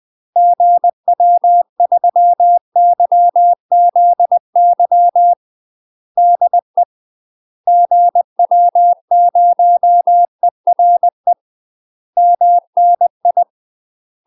Morse code, heard as a keyed tone, about 20 wpm, 700 Hz. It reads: GW3YZY DE GW0ERE MNI
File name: UK MASTER QSOs TRAINING / 20WPM / UK MASTER QSOs 0068